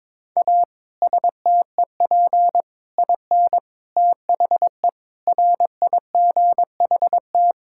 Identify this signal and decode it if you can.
Morse code, heard as a keyed tone, about 22 wpm, 700 Hz. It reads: A STEP IN THE RIGHT
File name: practice / 22wpm / Sets of 6 Words - 22wpm